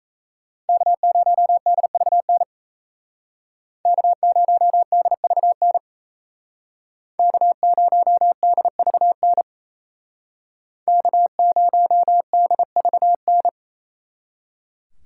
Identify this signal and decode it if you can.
Morse code, heard as a keyed tone, about 35 wpm, 700 Hz. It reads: K0DVN K0DVN K0DVN K0DVN